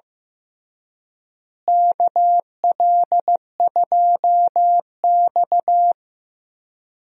Morse code, heard as a keyed tone, about 15 wpm, 700 Hz. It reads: KL2X
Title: USA Call Signs - 4 Characters - Repeated 3x - 15wpm